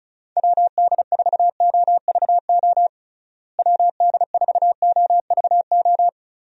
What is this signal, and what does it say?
Morse code, heard as a keyed tone, about 36 wpm, 700 Hz. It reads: WD4OVO WD4OVO
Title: USA Call Signs - 6 Characters - Repeated 2x - 35wpm